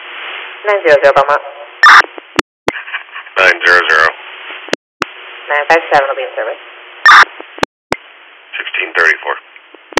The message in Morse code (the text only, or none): none